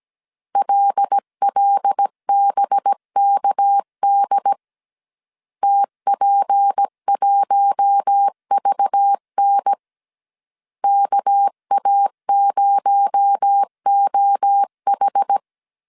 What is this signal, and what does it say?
Morse code, heard as a keyed tone, about 17 wpm, 800 Hz. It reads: LLBKD TP1VN KA0OH
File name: 16wpm